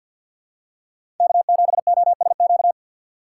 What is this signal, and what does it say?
Morse code, heard as a keyed tone, about 53 wpm, 700 Hz. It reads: K7OSQ
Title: USA Call Signs - 5 Characters - 3x Character Spacing - 50wpm